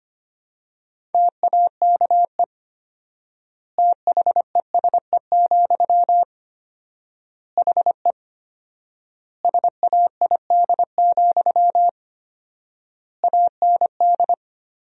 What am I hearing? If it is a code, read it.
Morse code, heard as a keyed tone, about 25 wpm, 700 Hz. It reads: TAKE THESE, HE SAID, AND
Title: 4x Word-Spacing - Sentences from Top 300 Words - 25wpm